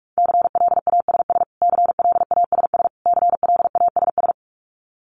Morse code, heard as a keyed tone, about 44 wpm, 700 Hz. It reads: CLASS CLASS CLASS
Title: Top 500 Words Repeated 45wpm